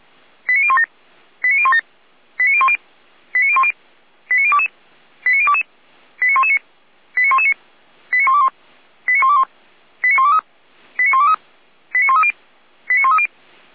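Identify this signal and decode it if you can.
Five-tone selective call (ZVEI), heard as quick runs of five tones, tones of about 70 ms. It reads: 89017 89017 89010 89010 89022 89022 89119 89119 89121 89121 89123 89123 89130 89130
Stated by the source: Five-tone ZVEI selective call received on an SDR.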